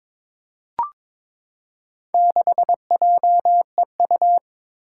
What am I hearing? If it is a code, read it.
Morse code, heard as a keyed tone, about 21 wpm, 700 Hz. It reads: 6JEU
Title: Four Challenging Letter-Number Combinations 22wpm